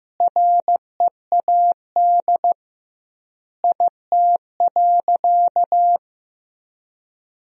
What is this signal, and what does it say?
Morse code, heard as a keyed tone, about 16 wpm, 700 Hz. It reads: READ IT.
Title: 2x Word-Spacing - Sentences from Top 500 Words - 15wpm